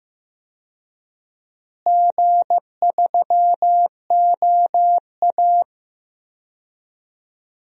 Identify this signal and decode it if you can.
Morse code, heard as a keyed tone, about 15 wpm, 700 Hz. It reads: G3OA
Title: DXCC-100 Call Signs - 4 Characters - Repeated 2x - 15wpm